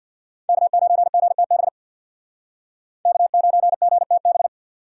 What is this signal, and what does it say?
Morse code, heard as a keyed tone, about 64 wpm, 700 Hz. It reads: K9GTB K9GTB